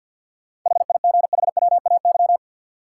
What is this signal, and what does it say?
Morse code, heard as a keyed tone, about 52 wpm, 700 Hz. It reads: HIGHWAY
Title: Sentences from Top 1400 Words - New Word Drill - 50wpm